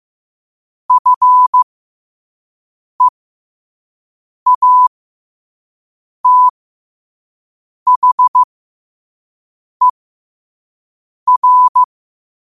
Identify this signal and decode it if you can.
Morse code, heard as a keyed tone, about 15 wpm, spaced as for 5 wpm, 1000 Hz. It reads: FEATHER
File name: qotdfiles / 5wpm